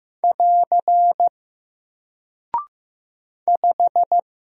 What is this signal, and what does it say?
Morse code, heard as a keyed tone, about 15 wpm, 700 Hz. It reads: <AR> 5
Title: Single Character (Letter:Number:Punctuation:Prosign) 15wpm